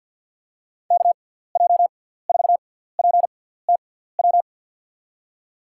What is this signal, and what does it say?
Morse code, heard as a keyed tone, about 49 wpm, spaced as for 17 wpm, 700 Hz. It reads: KJ4PTW
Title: USA Call Signs - 6x Character Spacing - 50wpm